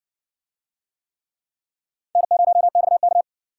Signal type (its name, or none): Morse code